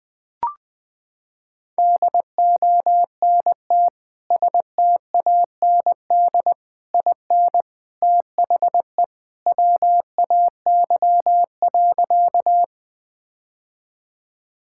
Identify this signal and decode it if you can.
Morse code, heard as a keyed tone, about 20 wpm, 700 Hz. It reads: DONT STAND IN THE WAY.
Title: Sentences with 3-Word Phrases - 20wpm